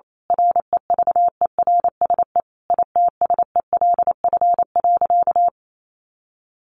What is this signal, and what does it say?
Morse code, heard as a keyed tone, about 28 wpm, 700 Hz. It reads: REVERSE ITSELF.
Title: A Sound Of Thunder 28wpm